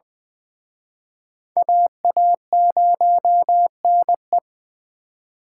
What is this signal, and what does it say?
Morse code, heard as a keyed tone, about 25 wpm, 700 Hz. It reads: AA0NE
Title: Speed Racing - USA 2x2 Call Signs 20wpm